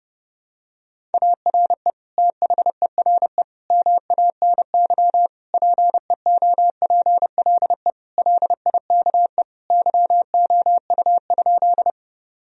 Morse code, heard as a keyed tone, about 32 wpm, 700 Hz. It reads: ARE THERE MANY PEOPLE LIKE YOU?
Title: Sentences from Top 1000 Words - Review Part 1 - 30wpm